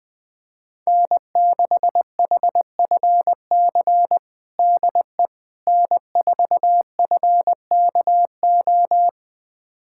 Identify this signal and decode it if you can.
Morse code, heard as a keyed tone, about 20 wpm, 700 Hz. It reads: N6HFC DE N4FKO
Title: USA Callsign DE Callsign - 5 Characters - 20wpm